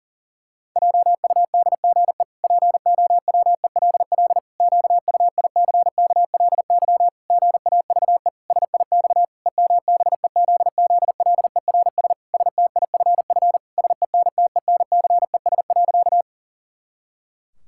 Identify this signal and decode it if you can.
Morse code, heard as a keyed tone, about 40 wpm, 700 Hz. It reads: JUDGE POWELL QUICKLY GAVE SIX EMBEZZLERS STIFF SENTENCES.
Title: Pangram Challenge v2 - 40wpm